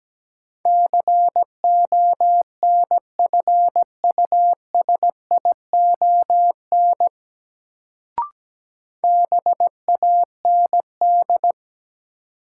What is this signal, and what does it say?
Morse code, heard as a keyed tone, about 17 wpm, 700 Hz. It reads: CONFUSION BAND
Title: Sentences from Top 1600 Words - New Word Drill - 17wpm